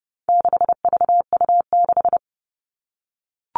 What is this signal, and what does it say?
Morse code, heard as a keyed tone, about 30 wpm, 700 Hz. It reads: 6VU6
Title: All Cut Numbers - 4-Digit 30wpm